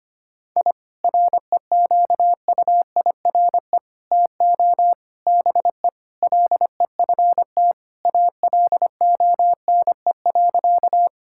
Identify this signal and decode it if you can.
Morse code, heard as a keyed tone, about 25 wpm, 700 Hz. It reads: I REQUIRE TO BE LEFT ALONE.
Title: Sentences from Top 1000 Words - Review Part 6 - 25wpm